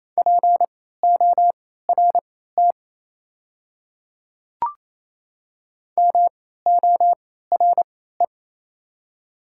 Morse code, heard as a keyed tone, about 28 wpm, spaced as for 16 wpm, 690 Hz. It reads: PORT MORE
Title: 3x Character Spacing - Top 200 Words - 28wpm